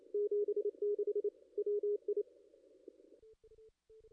Morse code, heard as a keyed tone, about 26 wpm, 440 Hz. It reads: 76 WI
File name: contests / ss-cw-k5tr-op-n5rz -2300